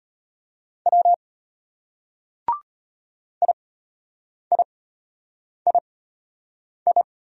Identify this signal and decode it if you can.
Morse code, heard as a keyed tone, about 31 wpm, spaced as for 8 wpm, 700 Hz. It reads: W IIII
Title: Speed-Racing Single Letters 25wpm